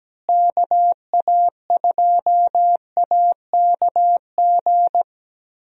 Morse code, heard as a keyed tone, about 17 wpm, 700 Hz. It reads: KA2AKG
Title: USA Call Signs - 6 Characters - Repeated 3x - 17wpm